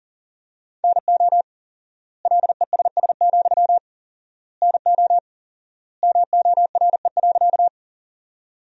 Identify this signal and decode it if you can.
Morse code, heard as a keyed tone, about 40 wpm, 700 Hz. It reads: NO LESS, NO MORE.
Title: 4x Word-Spacing - Sentences from Top 400 Words - 40wpm